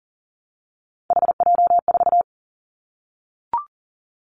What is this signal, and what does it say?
Morse code, heard as a keyed tone, about 40 wpm, 700 Hz. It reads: HJ4
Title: Callsign Prefixes for Top 100 DXCC - 40wpm